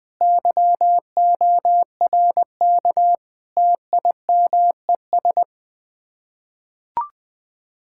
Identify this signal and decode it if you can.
Morse code, heard as a keyed tone, about 20 wpm, 700 Hz. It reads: YORK TIMES
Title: Build A Phrase - 5-Word Phrases - 20wpm